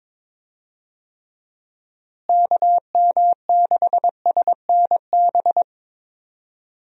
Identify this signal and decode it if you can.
Morse code, heard as a keyed tone, about 22 wpm, 700 Hz. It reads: KM6SNB
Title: USA Call Signs - 6 Characters - Repeated 2x - 22wpm